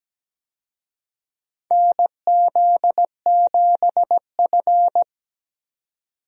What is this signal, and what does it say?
Morse code, heard as a keyed tone, about 17 wpm, 700 Hz. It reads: NZ7F
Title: USA Call Signs - 4 Characters - 17wpm